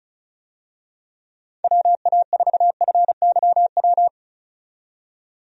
Morse code, heard as a keyed tone, about 34 wpm, 700 Hz. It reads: WA4FYW
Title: USA Call Signs - 6 Characters - 35wpm